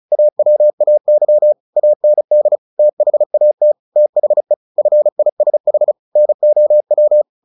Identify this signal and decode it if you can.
Morse code, heard as a keyed tone, about 36 wpm, 600 Hz. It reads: AWAY AND THAT THE FISH NOW
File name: the old man and the sea 2 35wpm